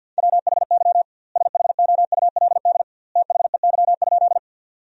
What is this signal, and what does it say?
Morse code, heard as a keyed tone, about 52 wpm, 700 Hz. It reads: WHY SHOULD THEY?
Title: 2x Word-Spacing - Sentences from Top 300 Words - 50wpm